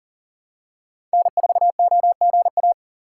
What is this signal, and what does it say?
Morse code, heard as a keyed tone, about 40 wpm, 700 Hz. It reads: N4OGA
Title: USA Call Signs - 5 Characters - 2x Character Spacing - 40wpm